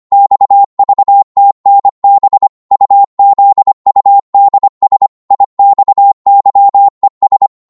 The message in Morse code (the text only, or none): XVTNB UZUDS IXYES